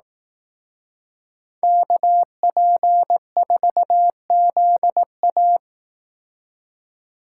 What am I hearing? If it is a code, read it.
Morse code, heard as a keyed tone, about 18 wpm, 700 Hz. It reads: KP4ZA
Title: Speed Racing - USA 2x2 Call Signs 15wpm